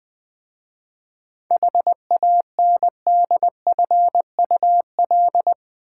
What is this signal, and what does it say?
Morse code, heard as a keyed tone, about 20 wpm, 700 Hz. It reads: HANDFUL
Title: 7 Letter Words - First 3 Letters Given - 20wpm